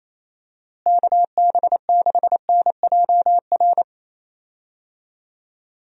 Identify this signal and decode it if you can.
Morse code, heard as a keyed tone, about 27 wpm, 700 Hz. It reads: KB6NJR
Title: USA Call Signs - 6 Characters - Repeated 3x - 28wpm